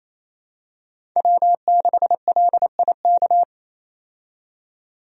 Morse code, heard as a keyed tone, about 28 wpm, 700 Hz. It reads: W6LIK
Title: USA Call Signs - 5 Characters - 28wpm